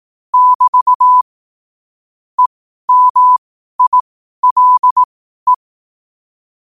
Morse code, heard as a keyed tone, about 18 wpm, spaced as for 12 wpm, 1000 Hz. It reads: <BT> EMILE